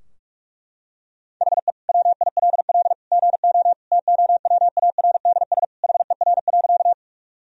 Morse code, heard as a keyed tone, about 43 wpm, 700 Hz. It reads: HE WILL GO TOWARDS HER.
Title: Sentences from Top 1000 Words - Review Part 2 - 45wpm